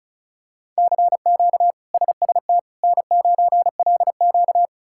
Morse code, heard as a keyed tone, about 36 wpm, 700 Hz. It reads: CQ SST N9LQ
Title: K1USN SST QSO Practice - 35wpm